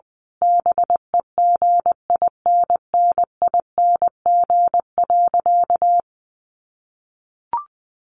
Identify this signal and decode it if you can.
Morse code, heard as a keyed tone, about 20 wpm, 700 Hz. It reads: BEGINNING.
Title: Sentences from Top 1500 Words 20wpm